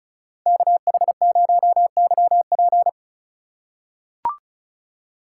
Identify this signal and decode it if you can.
Morse code, heard as a keyed tone, about 36 wpm, 700 Hz. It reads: KH0YP
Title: USA Call Signs - 4x Character Spacing - 35wpm